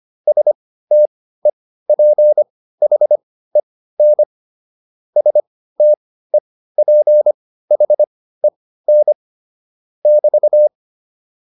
Morse code, heard as a keyed tone, about 25 wpm, spaced as for 15 wpm, 600 Hz. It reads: STEPHEN STEPHEN <BT>
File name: QSO107 15